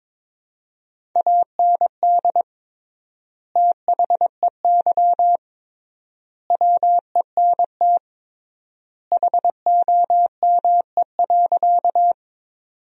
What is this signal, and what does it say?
Morse code, heard as a keyed tone, about 22 wpm, 700 Hz. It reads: AND THEY WENT HOME.
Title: 3x Word-Spacing - Sentences from Top 200 Words - 22wpm